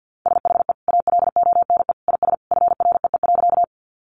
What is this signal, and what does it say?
Morse code, heard as a keyed tone, about 50 wpm, 700 Hz. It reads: SHE ALONE IS FREE.
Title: Sentences from Top 1000 Words 50wpm